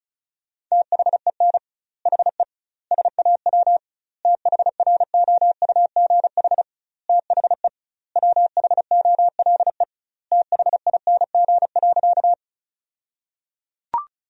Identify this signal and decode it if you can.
Morse code, heard as a keyed tone, about 34 wpm, 700 Hz. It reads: THEN HE SAW THROUGH THE WHOLE THING.
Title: 2x Word-Spacing - Sentences from Top 400 Words - 35wpm